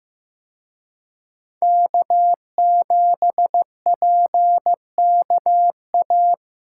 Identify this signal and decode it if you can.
Morse code, heard as a keyed tone, about 18 wpm, 700 Hz. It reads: K7PKA
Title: Speed Racing - USA 1x3 Call Signs 15wpm